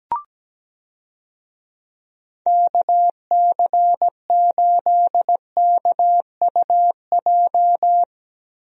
Morse code, heard as a keyed tone, about 17 wpm, 700 Hz. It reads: KC8KUJ
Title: USA Call Signs - 6 Characters - Repeated 3x - 17wpm